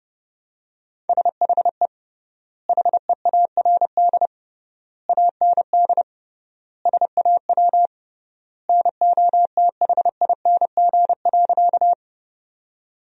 Morse code, heard as a keyed tone, about 30 wpm, 700 Hz. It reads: SHE HEARD AND SAW NOTHING.
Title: 3x Word-Spacing - Sentences from Top 500 Words - 30wpm